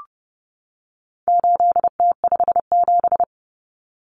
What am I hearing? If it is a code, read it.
Morse code, heard as a keyed tone, about 30 wpm, 700 Hz. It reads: 8T57
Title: Common Cut Numbers - 4-Digit 30wpm